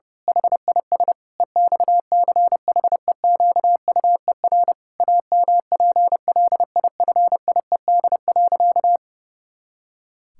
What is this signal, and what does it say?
Morse code, heard as a keyed tone, about 30 wpm, 700 Hz. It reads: HIS EXCHEQUER AMPLIFIED.
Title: Pangram Challenge v2 - 30wpm